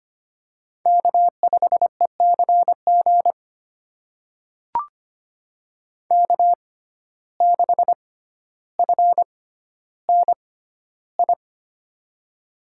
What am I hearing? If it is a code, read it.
Morse code, heard as a keyed tone, about 25 wpm, spaced as for 13 wpm, 700 Hz. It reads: K5ECG K6FNI
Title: USA Call Signs - 5 Characters - 6x Character Spacing - 25wpm